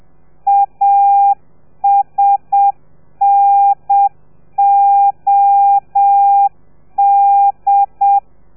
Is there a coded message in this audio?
Yes: Morse code from 0.5 to 8.2 s, heard as a keyed tone, about 7 wpm, 800 Hz. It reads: ASNOD